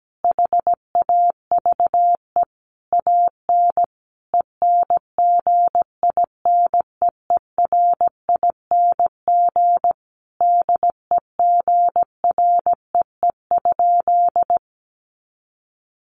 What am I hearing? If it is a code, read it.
Morse code, heard as a keyed tone, about 17 wpm, 700 Hz. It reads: HAVE AN ENGINEERING DEGREE?
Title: Sentences from Top 1500 Words 17wpm